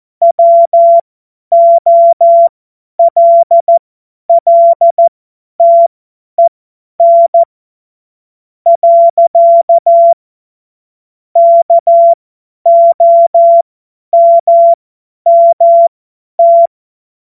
Morse code, heard as a keyed tone, about 14 wpm, spaced as for 9 wpm, 680 Hz. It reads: WOLLTEN. KOMMT
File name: Morsekurs / Witze / Witz 10. Mai 2022 50 F70